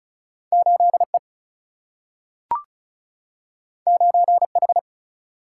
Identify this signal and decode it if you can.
Morse code, heard as a keyed tone, about 36 wpm, 700 Hz. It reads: 8 E 9 H
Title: Single-Letter Single-Number 35wpm